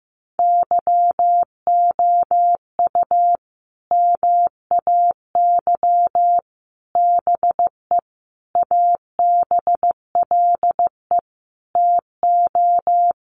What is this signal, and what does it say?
Morse code, heard as a keyed tone, about 15 wpm, 700 Hz. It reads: YOU MAY BE ABLE TO
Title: Sets of 5 Words Encore 15wpm